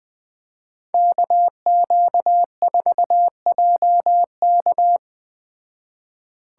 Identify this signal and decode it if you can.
Morse code, heard as a keyed tone, about 20 wpm, 700 Hz. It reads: KQ4JK